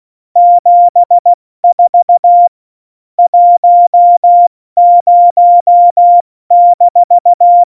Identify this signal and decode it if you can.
Morse code, heard as a keyed tone, about 16 wpm, 700 Hz. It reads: 74 10-
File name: Morsekurs / Witze / Witz 15. Oktober 2019, 70 F80